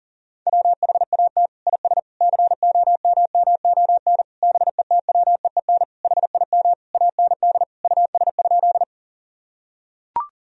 Morse code, heard as a keyed tone, about 37 wpm, 700 Hz. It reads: WHAT IS COMMON BETWEEN HIM AND US?
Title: Sentences from Top 1000 Words - Review Part 3 - 40wpm